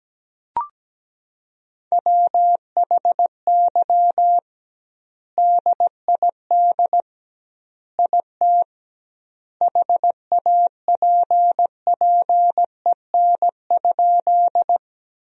Morse code, heard as a keyed tone, about 17 wpm, 700 Hz. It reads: WHY DID IT HAPPEN?
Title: 2x Word-Spacing - Sentences from Top 400 Words - 17wpm